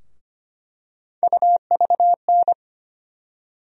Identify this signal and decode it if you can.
Morse code, heard as a keyed tone, about 25 wpm, 700 Hz. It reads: UVN
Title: USA Call Sign Postfixes - 3 Letters 25wpm